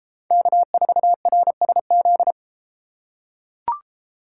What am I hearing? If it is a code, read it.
Morse code, heard as a keyed tone, about 30 wpm, 700 Hz. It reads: K4RSZ